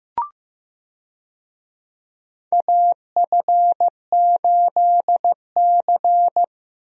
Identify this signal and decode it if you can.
Morse code, heard as a keyed tone, about 15 wpm, 700 Hz. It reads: AF8C
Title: USA Call Signs - 4 Characters - 15wpm